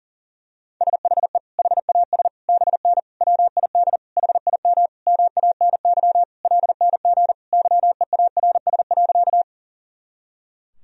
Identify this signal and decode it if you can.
Morse code, heard as a keyed tone, about 40 wpm, 700 Hz. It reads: SHE HAS BN WID HIM MANY LNG YEARS.
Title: Sentences from Top 200 Words and Abbreviations 40wpm